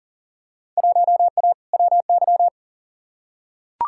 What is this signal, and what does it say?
Morse code, heard as a keyed tone, about 40 wpm, 700 Hz. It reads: 1A WY
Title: Field Day Exchanges - 40wpm